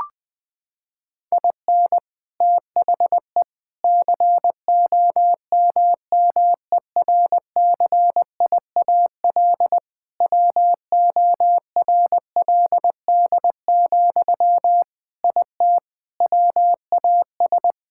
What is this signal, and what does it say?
Morse code, heard as a keyed tone, about 21 wpm, 700 Hz. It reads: IN THE COMMERCIAL WORLD, IT WAS
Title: Sentences from Top 1000 Words - Review Part 5 - 20wpm